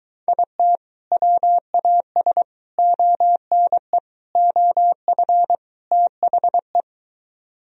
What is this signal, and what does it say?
Morse code, heard as a keyed tone, about 23 wpm, 700 Hz. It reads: IT WAS ONE OF THE